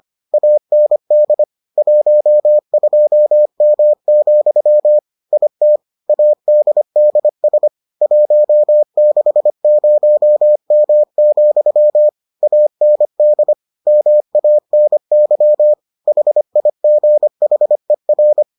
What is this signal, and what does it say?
Morse code, heard as a keyed tone, about 25 wpm, 600 Hz. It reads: AND 12M, IT ADDS 160M, AND MANY HIGHER